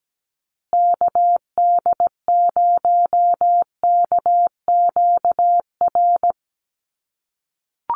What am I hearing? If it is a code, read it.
Morse code, heard as a keyed tone, about 17 wpm, 700 Hz. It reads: KD0KQR
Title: USA Call Signs Repeated 2x 17wpm